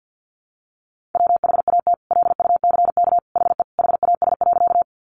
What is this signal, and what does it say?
Morse code, heard as a keyed tone, about 50 wpm, 700 Hz. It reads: WHAT LUCK HE HAS.
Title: Sentences from Top 1500 Words 50wpm